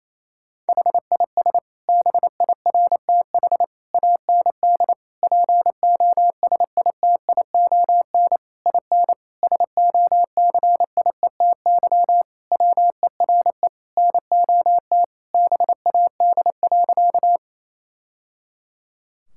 Morse code, heard as a keyed tone, about 27 wpm, 700 Hz. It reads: HIS BIRTH AND POSITION IN SOCIETY WERE NOT BAD.
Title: Sentences from Top 1000 Words - Review Part 5 - 28wpm